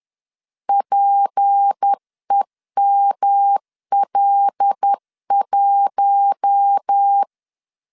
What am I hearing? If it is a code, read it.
Morse code, heard as a keyed tone, about 10 wpm, 800 Hz. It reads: PEML1
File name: LEZIONI / 6 PKX0 / 6 P 10wpm 2432